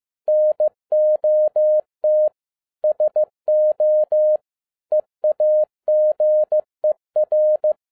Morse code, heard as a keyed tone, about 15 wpm, 610 Hz. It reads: NOT SO EAGER